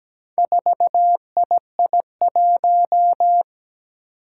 Morse code, heard as a keyed tone, about 17 wpm, 700 Hz. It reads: 4II1
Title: Four Challenging Letter-Number Combinations 17wpm